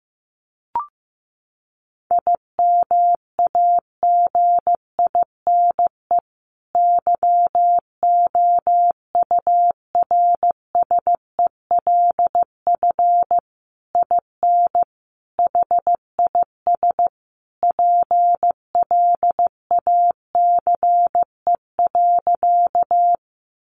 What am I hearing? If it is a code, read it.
Morse code, heard as a keyed tone, about 15 wpm, 700 Hz. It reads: IMAGINE YOURSELF IN HIS PLACE.
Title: Sentences from Top 1400 Words 15wpm